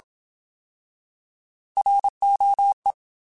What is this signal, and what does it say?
Morse code, heard as a keyed tone, about 27 wpm, 800 Hz. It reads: ROE
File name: Alphabetical / Alphabetical mp3s / Rs @25